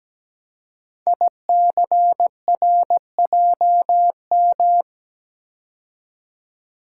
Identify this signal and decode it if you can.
Morse code, heard as a keyed tone, about 17 wpm, 700 Hz. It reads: ICRJM
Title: Code Groups - 5 Letters - 17wpm